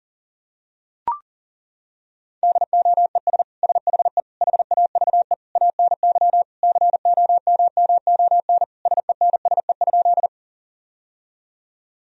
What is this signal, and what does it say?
Morse code, heard as a keyed tone, about 37 wpm, 700 Hz. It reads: DOES SHE HAVE ANY COMMON SENSE?
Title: Sentences from Top 1000 Words - Review Part 4 - 40wpm